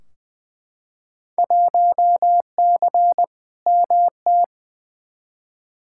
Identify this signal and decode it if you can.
Morse code, heard as a keyed tone, about 20 wpm, 700 Hz. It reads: 1C MT
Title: Field Day Exchanges Encore - 20wpm